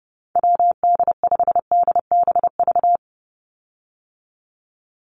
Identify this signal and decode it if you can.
Morse code, heard as a keyed tone, about 30 wpm, 700 Hz. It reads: WD5DBV
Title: Call Signs - 30wpm